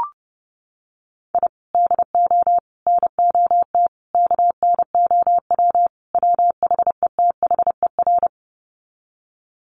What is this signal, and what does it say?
Morse code, heard as a keyed tone, about 30 wpm, 700 Hz. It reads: I DO NOT KNOW WHETHER
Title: Sets of 5 Words 30wpm